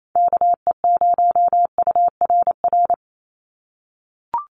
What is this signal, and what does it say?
Morse code, heard as a keyed tone, about 28 wpm, 700 Hz. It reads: KE0URR
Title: USA Call Signs 28wpm